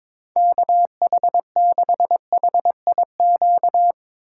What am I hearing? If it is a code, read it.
Morse code, heard as a keyed tone, about 22 wpm, 700 Hz. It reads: KH6HIQ
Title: USA Call Signs - 4x Character Spacing - 22wpm